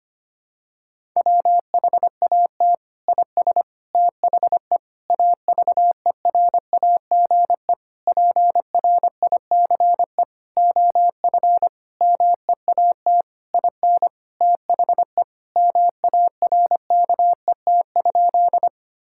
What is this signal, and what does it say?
Morse code, heard as a keyed tone, about 25 wpm, 700 Hz. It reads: WHAT IS THE AVERAGE PRICE OF MEAT IN THE MARKET?
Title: Sentences from Top 1000 Words - Review Part 6 - 25wpm